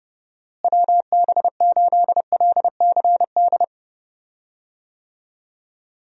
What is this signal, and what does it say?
Morse code, heard as a keyed tone, about 31 wpm, 700 Hz. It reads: WB8LCD
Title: USA Call Signs - 6 Characters - Repeated 2x - 30wpm